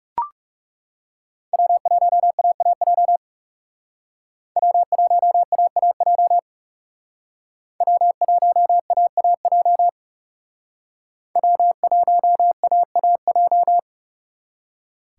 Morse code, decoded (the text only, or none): W1AAJ W1AAJ W1AAJ W1AAJ